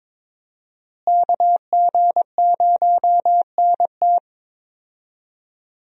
Morse code, heard as a keyed tone, about 22 wpm, 700 Hz. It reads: KG0NT
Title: USA Call Signs - 5 Characters - Repeated 2x - 22wpm